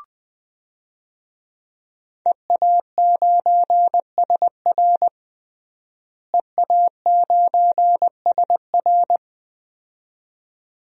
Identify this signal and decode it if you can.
Morse code, heard as a keyed tone, about 20 wpm, 700 Hz. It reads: EA9SR EA9SR
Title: DXCC-100 Call Signs - 5 Characters - Repeated 2x - 20wpm